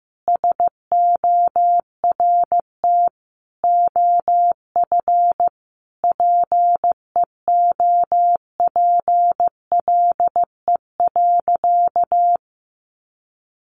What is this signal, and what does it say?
Morse code, heard as a keyed tone, about 15 wpm, 700 Hz. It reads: SORT OF PEOPLE.
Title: Sentences from Top 1300 Words 15wpm